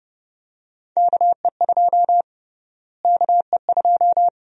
Morse code, heard as a keyed tone, about 30 wpm, 700 Hz. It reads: KE2 KE2
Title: USA Call Sign Prefixes - Repeated 2x - 30wpm